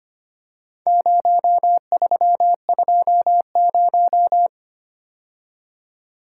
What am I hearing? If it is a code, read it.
Morse code, heard as a keyed tone, about 25 wpm, 700 Hz. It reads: 0320
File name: practice / 25wpm / Four Letter-Number Combinations - 25wpm